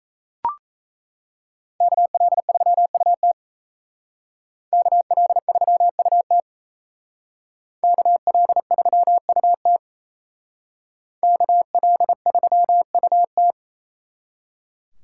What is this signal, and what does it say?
Morse code, heard as a keyed tone, about 37 wpm, 700 Hz. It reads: KL3UT KL3UT KL3UT KL3UT